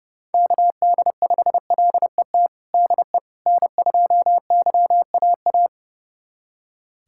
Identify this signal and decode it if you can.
Morse code, heard as a keyed tone, about 30 wpm, 700 Hz. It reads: KD5LET DE N2YAA
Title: USA Callsign DE Callsign 30wpm